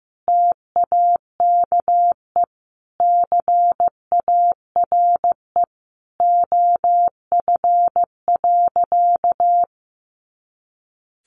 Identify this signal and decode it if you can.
Morse code, heard as a keyed tone, about 15 wpm, 700 Hz. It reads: TAKE CARE OF.
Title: Easy Phrases 15wpm